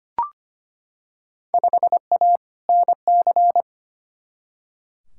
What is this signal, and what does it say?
Morse code, heard as a keyed tone, about 25 wpm, 700 Hz. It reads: 5A NC
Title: Field Day Exchanges - 25wpm